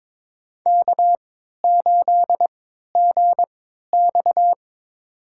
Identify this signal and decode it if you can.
Morse code, heard as a keyed tone, about 22 wpm, spaced as for 12 wpm, 700 Hz. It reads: K8GX
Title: USA Call Signs - 4 Characters - 3x Character Spacing - 22wpm